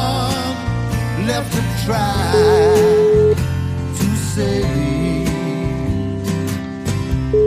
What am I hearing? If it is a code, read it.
Ringback tone (Europe, 425 Hz), with noise.